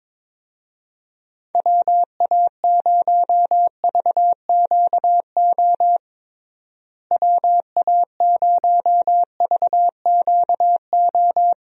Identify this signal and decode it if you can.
Morse code, heard as a keyed tone, about 22 wpm, 700 Hz. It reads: WA0VQO WA0VQO
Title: USA Call Signs - 6 Characters - Repeated 3x - 22wpm